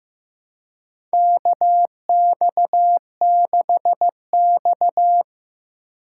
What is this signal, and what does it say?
Morse code, heard as a keyed tone, about 15 wpm, 700 Hz. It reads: KX6X
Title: K1USN SST QSO Practice - 15wpm